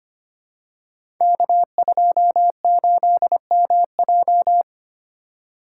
Morse code, heard as a keyed tone, about 25 wpm, 700 Hz. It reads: K28MJ
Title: Code Groups - 5 Letter:Numbers - 25wpm